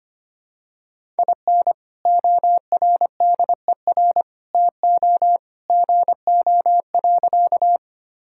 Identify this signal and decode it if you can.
Morse code, heard as a keyed tone, about 25 wpm, 700 Hz. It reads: IN ORDER TO GO.
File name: practice / 25wpm / Sentences with 2-Word Phrases - 25wpm